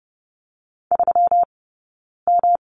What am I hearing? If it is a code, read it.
Morse code, heard as a keyed tone, about 30 wpm, 700 Hz. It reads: 3 M
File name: Lesson 20d - Number 3 - Two Characters 30wpm